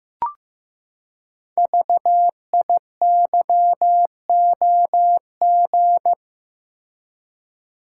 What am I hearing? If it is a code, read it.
Morse code, heard as a keyed tone, about 15 wpm, 700 Hz. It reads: VIYOG
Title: Code Groups - 5 Letters - 15wpm